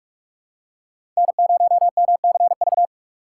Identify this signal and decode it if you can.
Morse code, heard as a keyed tone, about 46 wpm, 700 Hz. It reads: N0MCV
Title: USA Call Signs - 5 Characters - 3x Character Spacing - 45wpm